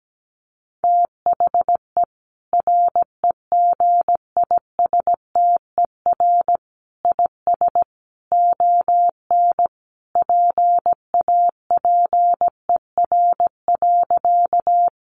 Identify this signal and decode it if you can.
Morse code, heard as a keyed tone, about 17 wpm, 700 Hz. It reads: THE REGISTER IS ON PAPER.
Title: Sentences from Top 1200 Words 17wpm